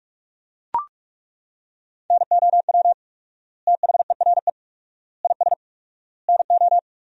Morse code, heard as a keyed tone, about 41 wpm, 700 Hz. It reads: NOW THERE IS NO
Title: 4x Word-Spacing - Sentences from Top 100 Words - 45wpm